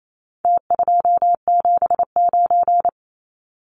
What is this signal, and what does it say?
Morse code, heard as a keyed tone, about 28 wpm, 700 Hz. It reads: T279
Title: Numbers - Contest Style - 4-Digit 28wpm